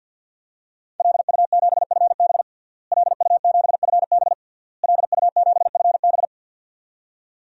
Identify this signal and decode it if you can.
Morse code, heard as a keyed tone, about 52 wpm, 700 Hz. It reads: LU7FB LU7FB LU7FB
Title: DXCC-100 Call Signs - 5 Characters - Repeated 3x - 50wpm